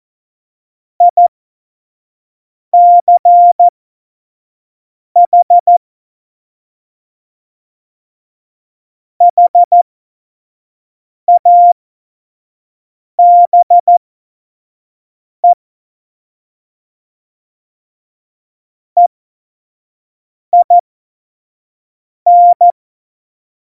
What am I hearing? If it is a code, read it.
Morse code, heard as a keyed tone, about 14 wpm, spaced as for 5 wpm, 700 Hz. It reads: ICH HABE EIN
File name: Morsekurs / Witze / Witz 05 Mai 2020, 25 F70